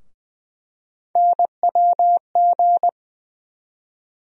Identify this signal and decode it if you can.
Morse code, heard as a keyed tone, about 20 wpm, 700 Hz. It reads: NWG
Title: USA Call Sign Postfixes - 3 Letters 20wpm